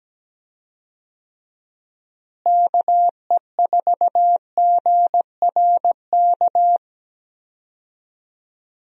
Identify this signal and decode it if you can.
Morse code, heard as a keyed tone, about 17 wpm, 700 Hz. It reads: KE4GRK
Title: USA Call Signs - 6 Characters - Repeated 2x - 17wpm